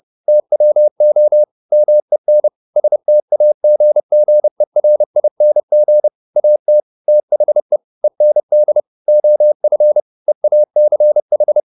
Morse code, heard as a keyed tone, about 30 wpm, 600 Hz. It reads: TWO MEN STAGGERING AT THE END OF EACH
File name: the old man and the sea / the old man and the sea 1 30wpm